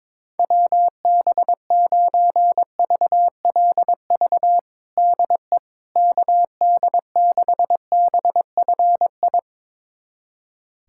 Morse code, heard as a keyed tone, about 22 wpm, 700 Hz. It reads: WB9VLV DE KD6BFI
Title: USA Callsign DE Callsign 22wpm